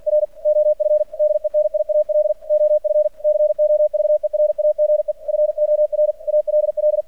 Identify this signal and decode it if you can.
Morse code, heard as a keyed tone, about 50 wpm, 600 Hz. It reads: R OF DENIAL OF COVERAGE FOR ALL
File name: archived / CRSnet